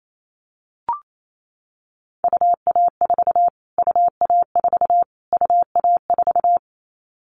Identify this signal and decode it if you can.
Morse code, heard as a keyed tone, about 28 wpm, 700 Hz. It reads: UA4 UA4 UA4
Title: Callsign Prefixes for Top 25 DXCC - Repeated 3x - 28wpm